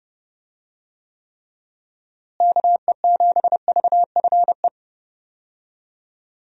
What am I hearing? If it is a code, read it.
Morse code, heard as a keyed tone, about 30 wpm, 700 Hz. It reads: KE7VFE
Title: USA Call Signs - 6 Characters - Repeated 3x - 30wpm